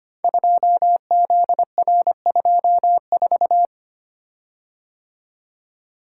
Morse code, heard as a keyed tone, about 25 wpm, 700 Hz. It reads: 2ZR24
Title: Code Groups - 5 Letter:Numbers - 25wpm